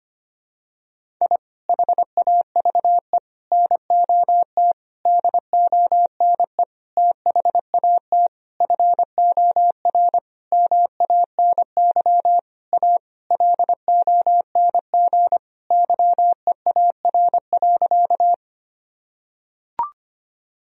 Morse code, heard as a keyed tone, about 25 wpm, 700 Hz. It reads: I HAVE NOT DONE THAT FOR MANY A LONG YEAR.
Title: Sentences from Top 1000 Words - Review Part 1 - 25wpm